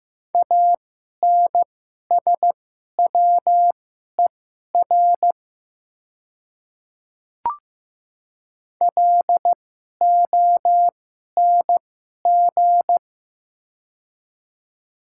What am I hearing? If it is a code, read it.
Morse code, heard as a keyed tone, about 15 wpm, spaced as for 11 wpm, 700 Hz. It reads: ANSWER LONG
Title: 2x Character Spacing - Top 300 Words - 15wpm